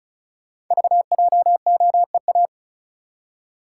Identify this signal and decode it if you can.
Morse code, heard as a keyed tone, about 36 wpm, 700 Hz. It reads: VJOEA
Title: Code Groups - 5 Letters - 35wpm